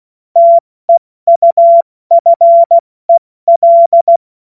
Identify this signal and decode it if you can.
Morse code, heard as a keyed tone, about 16 wpm, 680 Hz. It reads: TEUFEL
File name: Morsekurs / Witze / Witz 07. März 2023 70 F80